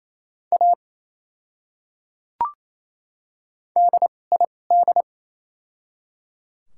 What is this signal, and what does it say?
Morse code, heard as a keyed tone, about 28 wpm, spaced as for 7 wpm, 700 Hz. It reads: ADID
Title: 2x Character Spacing - Top 100 Words - 28wpm